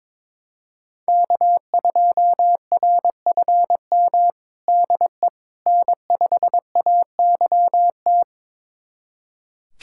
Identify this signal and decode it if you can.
Morse code, heard as a keyed tone, about 22 wpm, 700 Hz. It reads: K2RFM DE N5AYT
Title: USA Callsign DE Callsign - 5 Characters - 22wpm